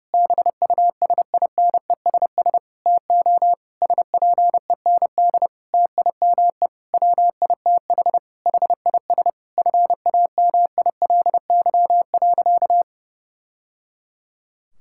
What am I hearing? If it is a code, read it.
Morse code, heard as a keyed tone, about 28 wpm, 700 Hz. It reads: BUSINESS TO SPEND TIME WITH HIS FAMILY.
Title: Sentences from Top 1000 Words - Review Part 6 - 30wpm